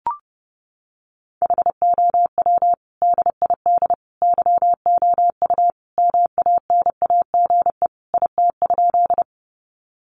Morse code, heard as a keyed tone, about 30 wpm, 700 Hz. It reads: HOW DID YOU MANAGE IT?
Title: Sentences from Top 1400 Words 30wpm